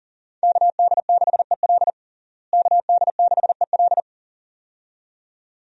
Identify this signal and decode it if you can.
Morse code, heard as a keyed tone, about 37 wpm, 700 Hz. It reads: KD6EL KD6EL
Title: USA Call Signs - 5 Characters - Repeated 2x - 40wpm